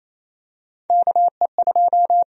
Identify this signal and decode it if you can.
Morse code, heard as a keyed tone, about 27 wpm, 700 Hz. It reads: KE2
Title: USA Call Sign Prefixes - Repeated 2x - 28wpm